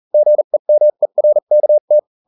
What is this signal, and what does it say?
Morse code, heard as a keyed tone, about 40 wpm, spaced as for 35 wpm, 600 Hz. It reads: GEMERKT